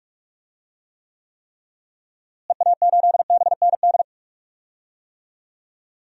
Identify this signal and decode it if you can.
Morse code, heard as a keyed tone, about 43 wpm, 700 Hz. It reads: EA8BND
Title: DXCC-100 Call Signs - 6 Characters - Repeated 3x - 45wpm